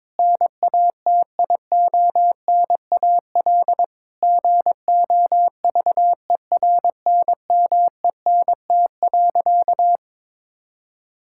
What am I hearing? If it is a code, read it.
Morse code, heard as a keyed tone, about 22 wpm, 700 Hz. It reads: NATIONAL GOVERNMENT.
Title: Sentences from Top 1000 Words - Review Part 5 - 22wpm